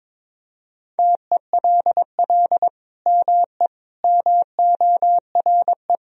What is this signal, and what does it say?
Morse code, heard as a keyed tone, about 22 wpm, 700 Hz. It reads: TELL ME MORE
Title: Sentences from Top 1000 Words - Review Part 2 - 22wpm